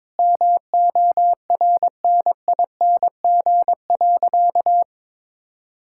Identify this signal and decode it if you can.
Morse code, heard as a keyed tone, about 22 wpm, 700 Hz. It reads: MORNING.
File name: practice / 22wpm / Sentences with 3-Word Phrases - 22wpm